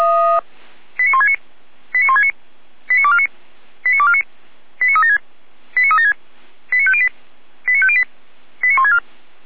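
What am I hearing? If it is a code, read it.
Five-tone selective call (ZVEI), heard as quick runs of five tones, tones of about 80 ms. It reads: 89169 89169 89249 89249 89376 89376 89508 89508 89164
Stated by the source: ZVEI five-tone call 89171 received by SDR.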